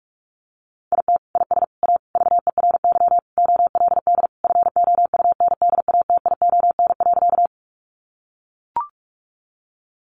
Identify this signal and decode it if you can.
Morse code, heard as a keyed tone, about 44 wpm, 700 Hz. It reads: IT IS A VERY OLD FOUNDATION.
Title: Sentences from Top 1200 Words 45wpm